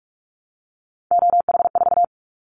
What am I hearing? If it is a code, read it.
Morse code, heard as a keyed tone, about 46 wpm, 690 Hz. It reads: OH4
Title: Callsign Prefixes for Top 50 DXCC - 45wpm